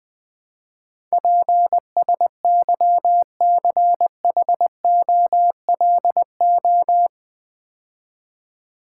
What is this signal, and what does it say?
Morse code, heard as a keyed tone, about 20 wpm, 700 Hz. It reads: PSYCHOLO
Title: Build A Word - 4 Letters At A Time - 12-Letter Words - 20wpm